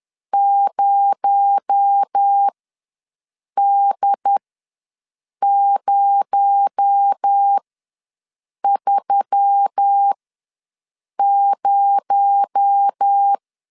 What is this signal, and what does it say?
Morse code, heard as a keyed tone, about 10 wpm, spaced as for 6 wpm, 800 Hz. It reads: 0D030
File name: LEZIONI / 6 PKX0 / 6 PKX0 ws 10wpm 4220